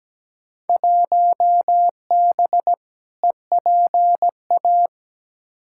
Morse code, heard as a keyed tone, about 17 wpm, 700 Hz. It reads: 1B EPA
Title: Field Day Exchanges Encore - 17wpm